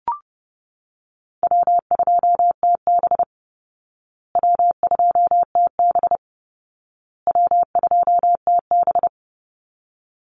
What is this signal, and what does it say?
Morse code, heard as a keyed tone, about 30 wpm, 700 Hz. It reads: W2TB W2TB W2TB
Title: Easy Call Signs Repeated 3x - 30wpm